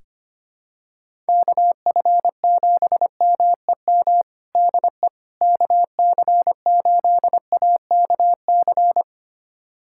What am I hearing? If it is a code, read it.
Morse code, heard as a keyed tone, about 25 wpm, 700 Hz. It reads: KF7MEM DE KC8AKC
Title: USA Callsign DE Callsign - 6 Characters - 25wpm